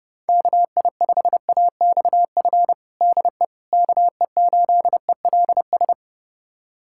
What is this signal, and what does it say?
Morse code, heard as a keyed tone, about 29 wpm, 700 Hz. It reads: KI5AXF DE KE8ELS
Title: USA Callsign DE Callsign - 6 Characters - 30wpm